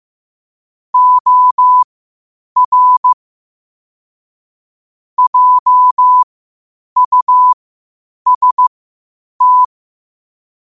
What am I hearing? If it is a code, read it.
Morse code, heard as a keyed tone, about 15 wpm, spaced as for 8 wpm, 1000 Hz. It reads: OR JUST
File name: 8wpm